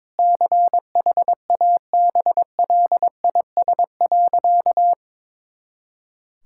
Morse code, heard as a keyed tone, about 22 wpm, 700 Hz. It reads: CHABLIS.
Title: Pangram Challenge v2 - 22wpm